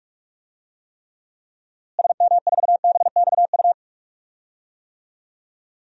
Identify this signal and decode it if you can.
Morse code, heard as a keyed tone, about 43 wpm, 700 Hz. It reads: SM4BXU
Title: DXCC-100 Call Signs - 6 Characters - Repeated 3x - 45wpm